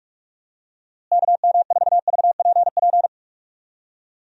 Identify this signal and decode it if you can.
Morse code, heard as a keyed tone, about 46 wpm, 700 Hz. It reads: KM4VPP
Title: USA Call Signs - 6 Characters - Repeated 3x - 45wpm